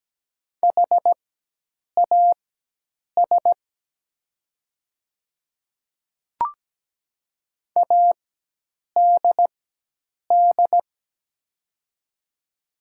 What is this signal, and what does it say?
Morse code, heard as a keyed tone, about 17 wpm, spaced as for 8 wpm, 700 Hz. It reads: HAS ADD
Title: 4x Character Spacing - Top 200 Words - 17wpm